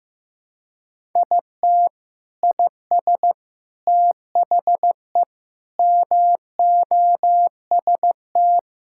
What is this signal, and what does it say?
Morse code, heard as a keyed tone, about 16 wpm, 700 Hz. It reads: IT IS THE MOST
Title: Sentences from Top 1000 Words - Review Part 6 - 15wpm